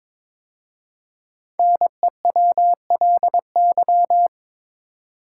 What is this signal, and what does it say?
Morse code, heard as a keyed tone, about 22 wpm, 690 Hz. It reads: NEWLY
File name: 5 Letter Words - First 3 Letters Given - 22wpm